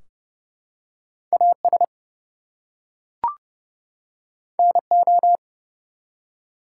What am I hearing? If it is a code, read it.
Morse code, heard as a keyed tone, about 30 wpm, 700 Hz. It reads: AS NO
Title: Two Letter Words - 30wpm